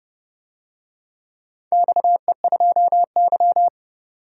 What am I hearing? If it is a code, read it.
Morse code, heard as a keyed tone, about 30 wpm, 700 Hz. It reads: XE2Y
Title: DXCC-100 Call Signs - 4 Characters - 30wpm